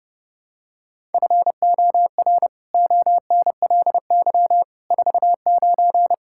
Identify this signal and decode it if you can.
Morse code, heard as a keyed tone, about 30 wpm, 700 Hz. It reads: FOR ONLY 49
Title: Pangram Challenge v2 - 30wpm